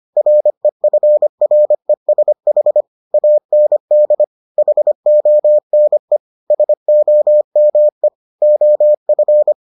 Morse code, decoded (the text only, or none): REFRESH AND HONE SOME OF